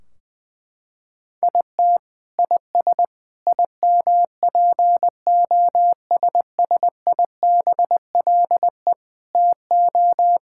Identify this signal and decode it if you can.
Morse code, heard as a keyed tone, about 20 wpm, 700 Hz. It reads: IT IS IMPOSSIBLE TO
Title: Sentences from Top 1600 Words 20wpm